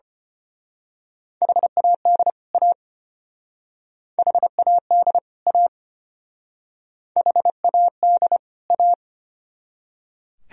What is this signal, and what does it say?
Morse code, heard as a keyed tone, about 29 wpm, 700 Hz. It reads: HAD A HAD A HAD A
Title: Speed-Racing - Sets of 2 Words 25wpm